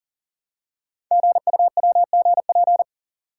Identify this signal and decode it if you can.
Morse code, heard as a keyed tone, about 43 wpm, 700 Hz. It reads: GUWGP
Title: Code Groups - 5 Letters - 40wpm